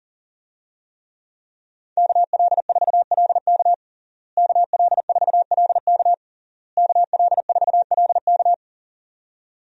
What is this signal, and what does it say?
Morse code, heard as a keyed tone, about 40 wpm, 700 Hz. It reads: KL4LK KL4LK KL4LK
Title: USA Call Signs - 5 Characters - Repeated 3x - 40wpm